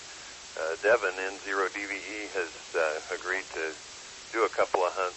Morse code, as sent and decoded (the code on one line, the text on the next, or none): none